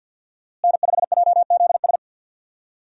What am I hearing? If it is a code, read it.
Morse code, heard as a keyed tone, about 52 wpm, 700 Hz. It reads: N5JZS
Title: USA Call Signs - 5 Characters - Repeated 2x - 50wpm